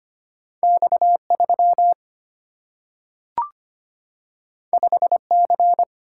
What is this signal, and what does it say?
Morse code, heard as a keyed tone, about 25 wpm, 700 Hz. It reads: X 3 5 C
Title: Single-Letter Single-Number 25wpm